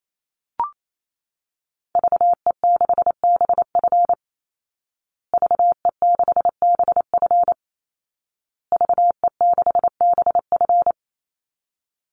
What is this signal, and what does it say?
Morse code, heard as a keyed tone, about 28 wpm, 700 Hz. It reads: VE6BF VE6BF VE6BF
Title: Call Signs Repeated 2x - 28wpm